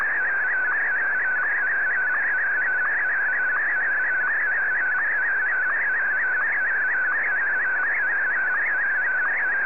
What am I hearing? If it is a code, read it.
SSTV (slow-scan television), PD 90.